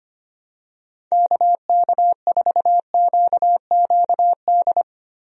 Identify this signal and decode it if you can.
Morse code, heard as a keyed tone, about 25 wpm, 700 Hz. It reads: KK4QQD
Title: USA Call Signs - 6 Characters - Repeated 2x - 25wpm